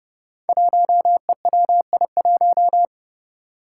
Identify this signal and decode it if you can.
Morse code, heard as a keyed tone, about 30 wpm, 700 Hz. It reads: 1EWI1
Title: Five Challenging Letter-Number Combinations 30wpm